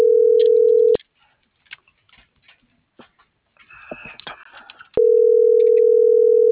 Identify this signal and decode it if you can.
Ringback tone (North America), 440 + 480 Hz.